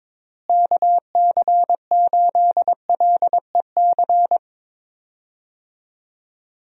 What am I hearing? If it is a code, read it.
Morse code, heard as a keyed tone, about 22 wpm, 700 Hz. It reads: KC8LEC
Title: USA Call Signs - 6 Characters - Repeated 3x - 22wpm